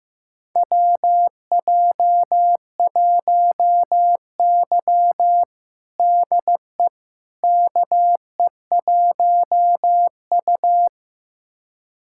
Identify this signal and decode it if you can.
Morse code, heard as a keyed tone, about 15 wpm, 700 Hz. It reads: WJ1Y DE KE1U
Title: USA Callsign DE Callsign - 4 Characters - 15wpm